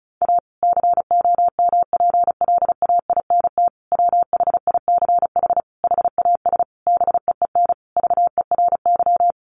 Morse code, heard as a keyed tone, about 34 wpm, 700 Hz. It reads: A COMPLAINT WHICH HAS BEEN VERY
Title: Sentences from Top 1400 Words 35wpm